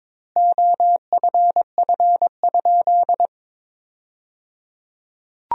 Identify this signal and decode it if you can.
Morse code, heard as a keyed tone, about 22 wpm, 700 Hz. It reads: OFF?
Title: 3x Word-Spacing - Sentences from Top 500 Words - 22wpm